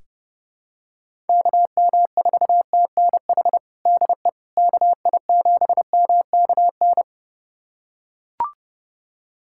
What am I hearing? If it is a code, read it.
Morse code, heard as a keyed tone, about 30 wpm, 700 Hz. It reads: KM4TNH DE KI7MKN
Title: USA Callsign DE Callsign - 6 Characters - 30wpm